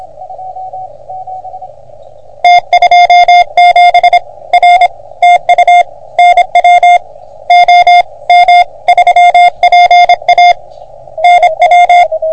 Morse code, heard as a keyed tone, about 26 wpm, 720 Hz. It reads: T27 R TU NW OM3PA NW
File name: OM3PA PILUP CUC 388